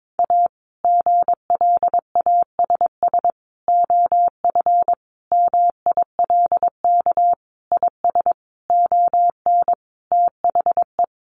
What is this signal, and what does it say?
Morse code, heard as a keyed tone, about 22 wpm, 700 Hz. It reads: A GLASS OF MILK IS ON THE
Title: Additional Easy Phrases 22wpm